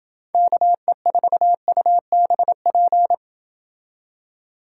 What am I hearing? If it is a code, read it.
Morse code, heard as a keyed tone, about 27 wpm, 700 Hz. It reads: KE4UBP